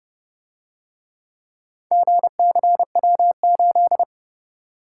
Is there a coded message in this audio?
Yes: Morse code from 1.9 to 4.0 s, heard as a keyed tone, about 26 wpm, 700 Hz. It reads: GCW8